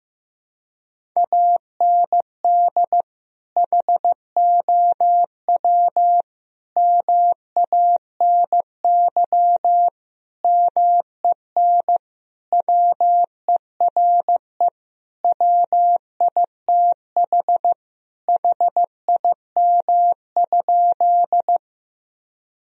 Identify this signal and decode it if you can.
Morse code, heard as a keyed tone, about 15 wpm, 700 Hz. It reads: AND HOW MANY MEN WERE WITH HIM?
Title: Sentences from Top 1000 Words - Review Part 1 - 15wpm